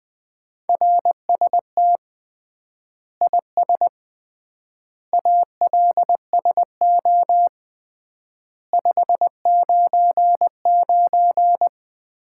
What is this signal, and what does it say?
Morse code, heard as a keyed tone, about 20 wpm, 700 Hz. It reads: RST IS ALSO 599
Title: Common QSO Phrases - 3x Word-Spacing - 20wpm